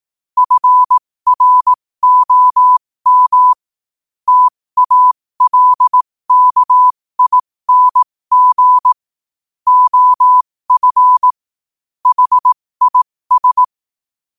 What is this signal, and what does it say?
Morse code, heard as a keyed tone, about 18 wpm, spaced as for 15 wpm, 1000 Hz. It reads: FROM TALKING OF HIS
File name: qotdfiles / 15wpm